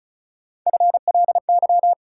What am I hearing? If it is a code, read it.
Morse code, heard as a keyed tone, about 34 wpm, 700 Hz. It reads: FLY
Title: Sentences from Top 500 Words - New Word Drill - 35wpm